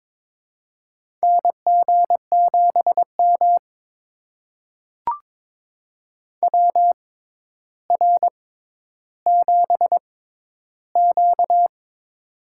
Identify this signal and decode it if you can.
Morse code, heard as a keyed tone, about 22 wpm, spaced as for 11 wpm, 700 Hz. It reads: NG7M WR7Q
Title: USA Call Signs - 4 Characters - 6x Character Spacing - 22wpm